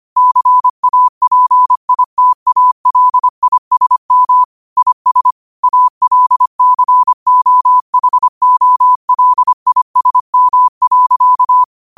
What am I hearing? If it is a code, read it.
Morse code, heard as a keyed tone, about 25 wpm, 1000 Hz. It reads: CAPITALISM IS ALCOHOLISM.
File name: qotdfiles / 25wpm